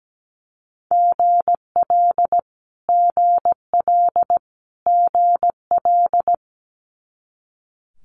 Morse code, heard as a keyed tone, about 17 wpm, 700 Hz. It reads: GL GL GL
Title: Multi-letter Phonemes Repeated 3x 17wpm